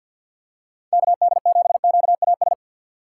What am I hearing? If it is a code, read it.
Morse code, heard as a keyed tone, about 46 wpm, 690 Hz. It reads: KD7QAS
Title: USA Call Signs - 6 Characters - 3x Character Spacing - 50wpm